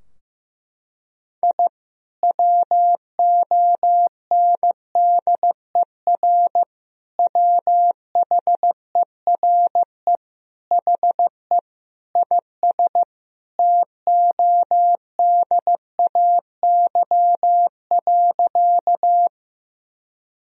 Morse code, heard as a keyed tone, about 16 wpm, 700 Hz. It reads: I WONDER WHERE HE IS TODAY.
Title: Sentences from Top 1000 Words - Review Part 3 - 15wpm